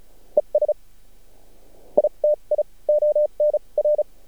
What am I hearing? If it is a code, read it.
Morse code, heard as a keyed tone, about 34 wpm, spaced as for 17 wpm, 600 Hz. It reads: ESITI ON R
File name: archived / CRSnet